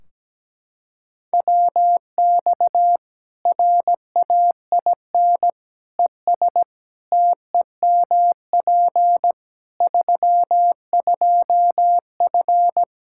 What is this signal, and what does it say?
Morse code, heard as a keyed tone, about 17 wpm, 700 Hz. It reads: WX RAIN ES TEMP 32F
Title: K1USN SST QSO Practice 17wpm